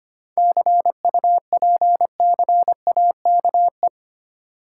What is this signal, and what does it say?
Morse code, heard as a keyed tone, about 25 wpm, 700 Hz. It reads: CUPCAKE
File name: practice / 25wpm / Difficult Words with Hints - 25wpm